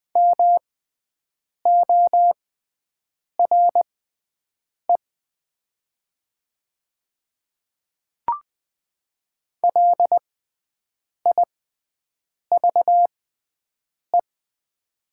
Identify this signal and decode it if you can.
Morse code, heard as a keyed tone, about 20 wpm, spaced as for 7 wpm, 700 Hz. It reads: MORE LIVE
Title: 6x Character Spacing - Top 300 Words - 20wpm